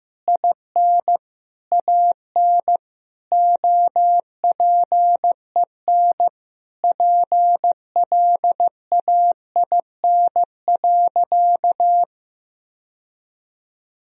Morse code, heard as a keyed tone, about 15 wpm, 700 Hz. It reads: IN AN OPEN PLAIN.
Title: Sentences from Top 1000 Words - Review Part 2 - 15wpm